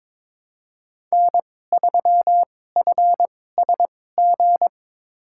Morse code, heard as a keyed tone, about 22 wpm, spaced as for 16 wpm, 700 Hz. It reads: N3FSG
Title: USA Call Signs - 2x Character Spacing - 22wpm